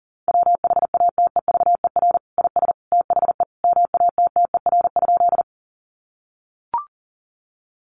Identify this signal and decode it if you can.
Morse code, heard as a keyed tone, about 40 wpm, 700 Hz. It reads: WHATEVER IS THE MATTER?
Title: Sentences from Top 1100 Words 40wpm